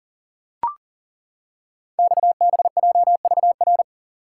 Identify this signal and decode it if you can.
Morse code, heard as a keyed tone, about 39 wpm, 690 Hz. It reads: XBJVR
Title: Code Groups - 5 Letters - 40wpm